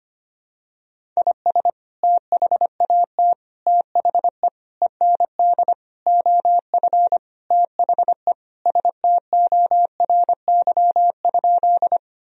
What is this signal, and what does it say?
Morse code, heard as a keyed tone, about 25 wpm, 700 Hz. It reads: IS THAT THE END OF THE STORY?
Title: Sentences from Top 1000 Words - Review Part 1 - 25wpm